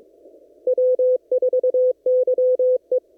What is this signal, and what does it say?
Morse code, heard as a keyed tone, about 22 wpm, 510 Hz. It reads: W4YE
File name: CW / kt5j wpx cw-op-k5pi